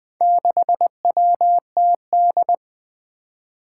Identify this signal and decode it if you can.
Morse code, heard as a keyed tone, about 20 wpm, 700 Hz. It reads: 6WTD
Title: Four Challenging Letter-Number Combinations 20wpm